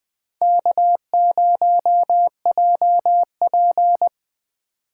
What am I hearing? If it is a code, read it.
Morse code, heard as a keyed tone, about 20 wpm, 700 Hz. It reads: K0JP
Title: USA Call Signs - 8x Character Spacing - 20wpm